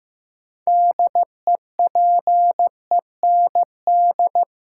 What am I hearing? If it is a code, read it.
Morse code, heard as a keyed tone, about 15 wpm, 700 Hz. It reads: DEPEND
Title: Sentences from Top 1300 Words - New Word Drill Alt - 15wpm